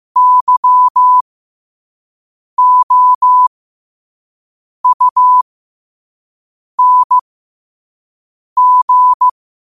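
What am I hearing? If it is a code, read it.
Morse code, heard as a keyed tone, about 15 wpm, spaced as for 5 wpm, 1000 Hz. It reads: YOUNG
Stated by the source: 5 wpm Morse code practice recording.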